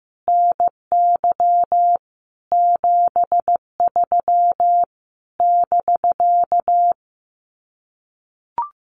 Morse code, heard as a keyed tone, about 15 wpm, 700 Hz. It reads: NY 73 <BK>
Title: Common QSO Phrases Encore 15wpm